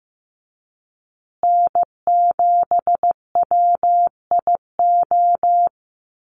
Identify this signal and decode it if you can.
Morse code, heard as a keyed tone, about 15 wpm, 700 Hz. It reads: N7WIO
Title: USA Call Signs Repeated 2x 15wpm